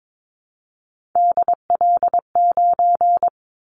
Morse code, heard as a keyed tone, about 22 wpm, 700 Hz. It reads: DL9
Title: Callsign Prefixes for Top 25 DXCC - Repeated 3x - 22wpm